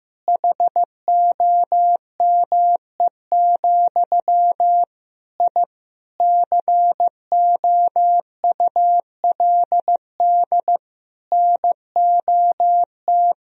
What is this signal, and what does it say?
Morse code, heard as a keyed tone, about 15 wpm, 700 Hz. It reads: HOME, I COULD NOT
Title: Sentences from Top 1000 Words - Review Part 3 - 15wpm